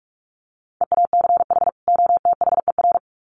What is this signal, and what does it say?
Morse code, heard as a keyed tone, about 44 wpm, 700 Hz. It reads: EACH OTHER
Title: Sets of 2 Words 45wpm